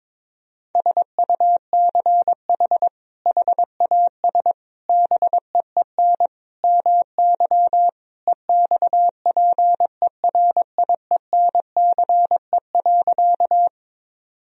Morse code, heard as a keyed tone, about 22 wpm, 700 Hz. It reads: SUCH HAS BEEN MY EXPERIENCE.
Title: Sentences from Top 1000 Words - Review Part 3 - 22wpm